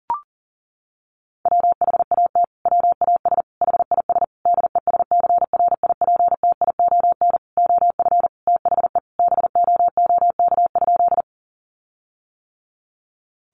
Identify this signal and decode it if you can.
Morse code, heard as a keyed tone, about 40 wpm, 700 Hz. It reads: WHAT WAS HIS DESCRIPTION OF THE BOOK?
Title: Sentences from Top 1300 Words 40wpm